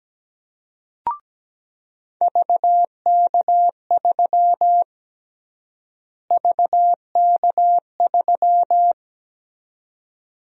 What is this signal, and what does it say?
Morse code, heard as a keyed tone, about 17 wpm, 700 Hz. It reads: VK3 VK3
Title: USA Call Sign Prefixes - Repeated 2x - 17wpm